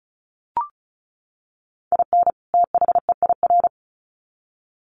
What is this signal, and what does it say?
Morse code, heard as a keyed tone, about 34 wpm, 700 Hz. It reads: IN THEIR
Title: Sets of 2 Words 35wpm